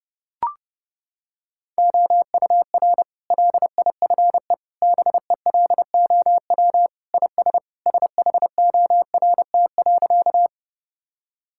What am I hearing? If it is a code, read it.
Morse code, heard as a keyed tone, about 30 wpm, 700 Hz. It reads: OUR LIFE BELOW IS SHORT.
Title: Sentences from Top 1000 Words - Review Part 5 - 30wpm